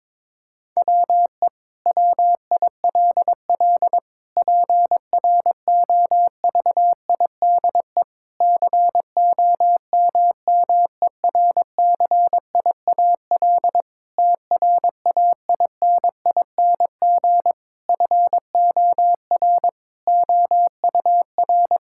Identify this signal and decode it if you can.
Morse code, heard as a keyed tone, about 22 wpm, 700 Hz. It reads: WE WILL PROVIDE COMMERCIAL TRAINING FOR OUR
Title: Sentences from Top 1000 Words - Review Part 5 - 22wpm